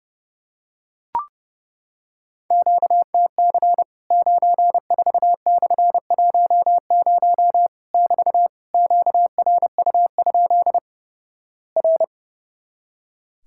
Morse code, heard as a keyed tone, about 30 wpm, 700 Hz. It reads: QTC 94/10 <BT> QRU? R
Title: Worked All Europe DX Contest - QTC Exchange - 30wpm